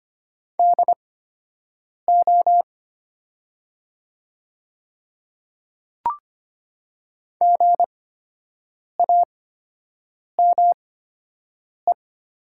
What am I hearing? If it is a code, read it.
Morse code, heard as a keyed tone, about 25 wpm, spaced as for 7 wpm, 700 Hz. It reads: DO GAME
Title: 8x Character Spacing - Top 500 Words - 25wpm